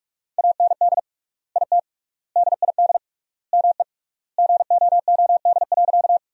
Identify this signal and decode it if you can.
Morse code, heard as a keyed tone, about 46 wpm, 700 Hz. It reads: AND IT DID ME GOOD.
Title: 3x Word-Spacing - Sentences from Top 200 Words - 45wpm